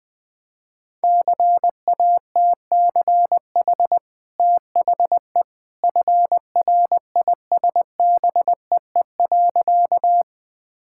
Morse code, heard as a keyed tone, about 20 wpm, 700 Hz. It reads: CATCH THE FRISBEE.
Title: Easy Sentences with Hints - 20wpm